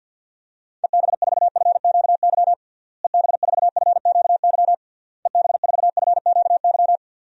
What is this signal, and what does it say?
Morse code, heard as a keyed tone, about 52 wpm, 700 Hz. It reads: EB4FQY EB4FQY EB4FQY
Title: DXCC-100 Call Signs - 6 Characters - Repeated 3x - 50wpm